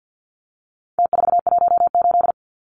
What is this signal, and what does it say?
Morse code, heard as a keyed tone, about 50 wpm, 700 Hz. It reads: T418
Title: Numbers - Contest Style - 4-Digit 50wpm